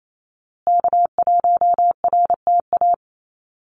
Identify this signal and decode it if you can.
Morse code, heard as a keyed tone, about 28 wpm, 700 Hz. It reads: K1RTA
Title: USA Call Signs Repeated 2x 28wpm